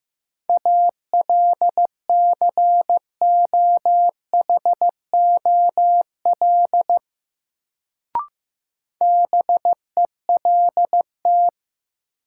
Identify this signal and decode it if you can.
Morse code, heard as a keyed tone, about 15 wpm, 700 Hz. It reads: ALCOHOL BELT
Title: Sentences from Top 1400 Words - New Word Drill Alt - 15wpm